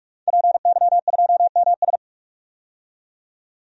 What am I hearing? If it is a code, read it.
Morse code, heard as a keyed tone, about 46 wpm, 700 Hz. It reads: PY2MS
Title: DXCC-100 Call Signs - 5 Characters - 45wpm